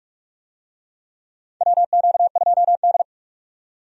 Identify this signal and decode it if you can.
Morse code, heard as a keyed tone, about 43 wpm, 700 Hz. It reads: WQ2D
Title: USA Call Signs - 4 Characters - 45wpm